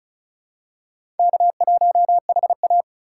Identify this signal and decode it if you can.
Morse code, heard as a keyed tone, about 34 wpm, 700 Hz. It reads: K1HA
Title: USA Call Signs - 4 Characters - 4x Character Spacing - 35wpm